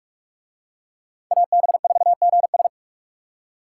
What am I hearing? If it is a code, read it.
Morse code, heard as a keyed tone, about 44 wpm, 700 Hz. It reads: AB4GS